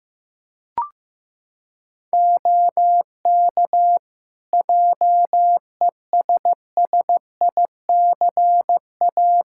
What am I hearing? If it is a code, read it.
Morse code, heard as a keyed tone, about 15 wpm, 700 Hz. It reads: OK JESSICA
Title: Practice QSO - Exchange 4 - Ending - 15wpm